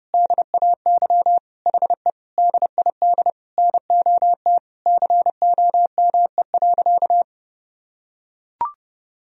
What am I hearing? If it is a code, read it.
Morse code, heard as a keyed tone, about 28 wpm, 700 Hz. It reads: DAY HE DID NOT COME.
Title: Sentences from Top 1000 Words - Review Part 4 - 30wpm